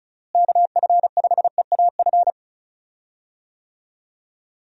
Morse code, heard as a keyed tone, about 36 wpm, 700 Hz. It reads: KF5EAF
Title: USA Call Signs - 6 Characters - Repeated 2x - 35wpm